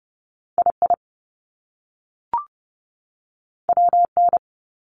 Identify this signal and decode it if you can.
Morse code, heard as a keyed tone, about 30 wpm, 700 Hz. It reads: II WN
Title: Top Two Letter Combinations in Words 30wpm